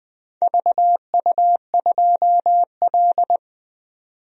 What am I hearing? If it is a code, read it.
Morse code, heard as a keyed tone, about 20 wpm, 700 Hz. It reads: VU2L
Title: DXCC-100 Call Signs - 4 Characters - 20wpm